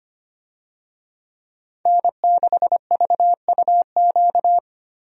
Morse code, heard as a keyed tone, about 25 wpm, 700 Hz. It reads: N6VUQ
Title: USA Call Signs - 5 Characters - 25wpm